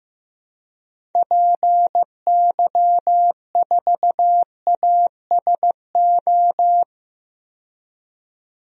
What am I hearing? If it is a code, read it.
Morse code, heard as a keyed tone, about 15 wpm, 700 Hz. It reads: PY4ASO
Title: DXCC-100 Call Signs - 6 Characters - 15wpm